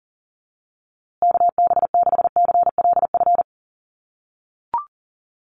Morse code, heard as a keyed tone, about 40 wpm, 700 Hz. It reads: KB6CLF
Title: USA Call Signs Repeated 3x 40wpm